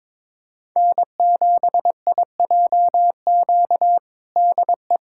Morse code, heard as a keyed tone, about 22 wpm, 700 Hz. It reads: N7IJQ DE
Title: USA Callsign DE Callsign 22wpm